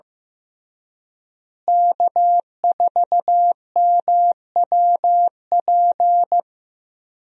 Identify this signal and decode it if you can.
Morse code, heard as a keyed tone, about 15 wpm, 700 Hz. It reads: K4MWP
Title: USA Call Signs - 5 Characters - Repeated 3x - 15wpm